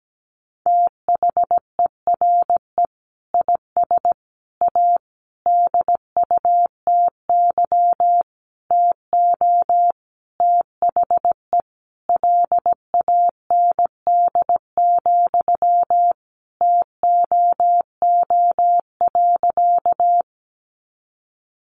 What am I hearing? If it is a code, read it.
Morse code, heard as a keyed tone, about 17 wpm, 700 Hz. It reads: THERE IS A DUTY TO THE LAND, TOO.
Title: Sentences from Top 1500 Words 17wpm